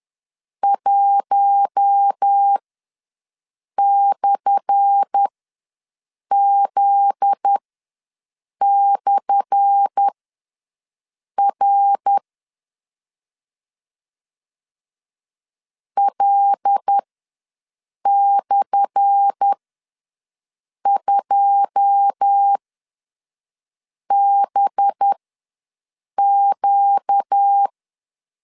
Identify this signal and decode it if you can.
Morse code, heard as a keyed tone, about 10 wpm, spaced as for 5 wpm, 800 Hz. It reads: 1/Z/R L/2BQ
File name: LEZIONI / 10 SIGNS / 10A SLASH ws 10wpm 4121